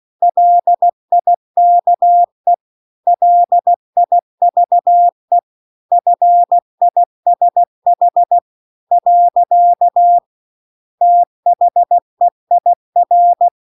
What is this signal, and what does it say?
Morse code, heard as a keyed tone, about 16 wpm, 700 Hz. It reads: LIKE LIVE FISH . THEIR
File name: CW-The-Big-Twoheaded-River-2-16WPM